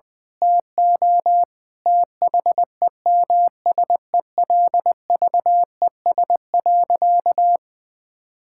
Morse code, heard as a keyed tone, about 20 wpm, 700 Hz. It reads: TO THEMSELVES.
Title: Sentences from Top 1000 Words - Review Part 6 - 20wpm